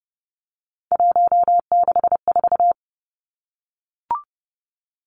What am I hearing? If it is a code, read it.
Morse code, heard as a keyed tone, about 30 wpm, 700 Hz. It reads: 164
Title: Numbers - 3-Digit 30wpm